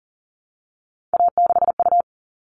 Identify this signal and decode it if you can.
Morse code, heard as a keyed tone, about 40 wpm, 700 Hz. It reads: A6U
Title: All Cut Numbers - 3-Digit 40wpm